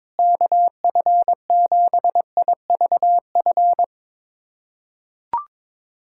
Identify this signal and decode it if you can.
Morse code, heard as a keyed tone, about 22 wpm, 700 Hz. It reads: KF7IVF
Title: USA Callsign DE Callsign 22wpm